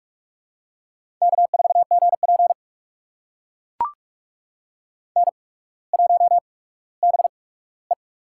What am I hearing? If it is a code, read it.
Morse code, heard as a keyed tone, about 44 wpm, spaced as for 19 wpm, 700 Hz. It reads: K4GP N1BE
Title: USA Call Signs - 4 Characters - 8x Character Spacing - 45wpm